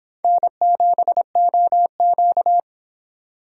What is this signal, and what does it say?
Morse code, heard as a keyed tone, about 25 wpm, 700 Hz. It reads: N7OQ